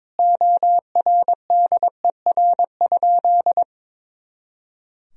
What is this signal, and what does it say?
Morse code, heard as a keyed tone, about 21 wpm, 700 Hz. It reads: ORDER?
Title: Sentences from Top 400 Words and Abbreviations 22wpm